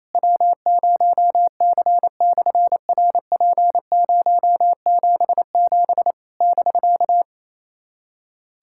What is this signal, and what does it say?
Morse code, heard as a keyed tone, about 29 wpm, 700 Hz. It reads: W0C/RP077 <BK>
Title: SOTA QSO Practice 28wpm